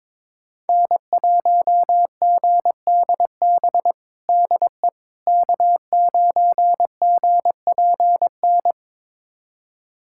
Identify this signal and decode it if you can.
Morse code, heard as a keyed tone, about 22 wpm, 700 Hz. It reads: N1GDB DE K9GPN
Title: USA Callsign DE Callsign - 5 Characters - 22wpm